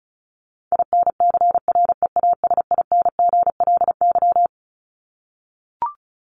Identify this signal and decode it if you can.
Morse code, heard as a keyed tone, about 34 wpm, 700 Hz. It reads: INCREASINGLY
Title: LY Suffix - Top 100 Words with Base Words 35wpm